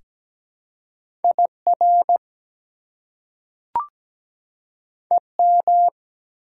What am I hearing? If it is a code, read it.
Morse code, heard as a keyed tone, about 17 wpm, 700 Hz. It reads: IR EM
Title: Prefixes and Suffixes 17wpm